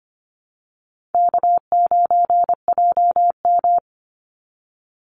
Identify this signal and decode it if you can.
Morse code, heard as a keyed tone, about 25 wpm, 700 Hz. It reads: K9JM
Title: Easy Call Signs Repeated 3x - 25wpm